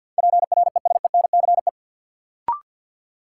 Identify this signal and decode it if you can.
Morse code, heard as a keyed tone, about 49 wpm, 700 Hz. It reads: PRESENCE
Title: Sets of 3 Words - New Word Drill - 50wpm